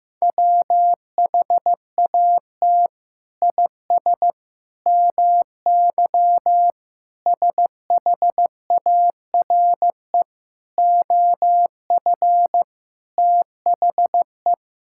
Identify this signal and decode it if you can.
Morse code, heard as a keyed tone, about 15 wpm, 700 Hz. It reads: WHAT IS MY SHARE OF THE
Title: Sentences from Top 1000 Words - Review Part 6 - 15wpm